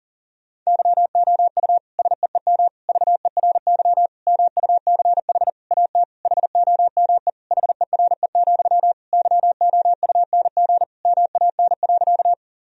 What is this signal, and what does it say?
Morse code, heard as a keyed tone, about 40 wpm, 700 Hz. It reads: YOU SEEM VERY MUCH AT HOME HERE, YOUNG MAN.
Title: Sentences from Top 1000 Words - Review Part 1 - 40wpm